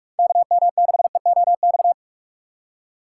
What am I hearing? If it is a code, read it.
Morse code, heard as a keyed tone, about 44 wpm, 700 Hz. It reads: KM6EOX